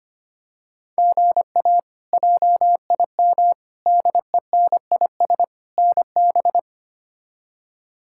Morse code, heard as a keyed tone, about 25 wpm, 700 Hz. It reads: GA JIM DENIS NB
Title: K1USN SST QSO Practice - 25wpm